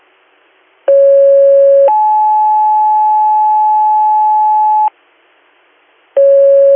Two-tone page, heard as a one-second tone, then a three-second tone, about 550 then 860 Hz. It is heard from 0.9 to 4.9 s.